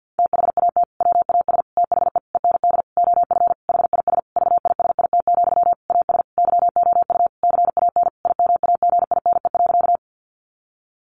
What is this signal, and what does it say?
Morse code, heard as a keyed tone, about 50 wpm, 700 Hz. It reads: THAT WAS THE END OF HIS VISIT, AS YOU CAN IMAGINE.
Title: Sentences from Top 1400 Words 50wpm